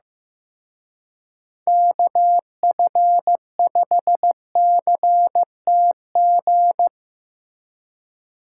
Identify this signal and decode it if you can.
Morse code, heard as a keyed tone, about 17 wpm, 700 Hz. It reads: KF5CTG
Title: Speed Racing - USA 2x3 Call Signs 15wpm